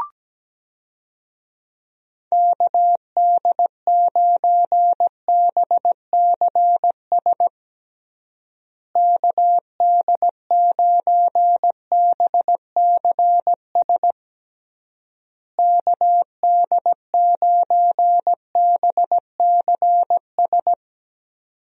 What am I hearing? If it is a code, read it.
Morse code, heard as a keyed tone, about 17 wpm, 700 Hz. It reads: KD9BCS KD9BCS KD9BCS
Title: USA Call Signs - 6 Characters - Repeated 3x - 17wpm